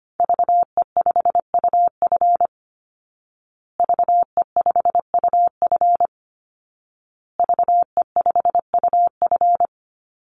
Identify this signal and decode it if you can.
Morse code, heard as a keyed tone, about 25 wpm, 700 Hz. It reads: VE5UF VE5UF VE5UF
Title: Call Signs Repeated 2x - 25wpm